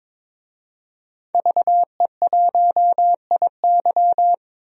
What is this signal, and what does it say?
Morse code, heard as a keyed tone, about 22 wpm, 700 Hz. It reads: VE1IY
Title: DXCC-100 Call Signs - 5 Characters - 22wpm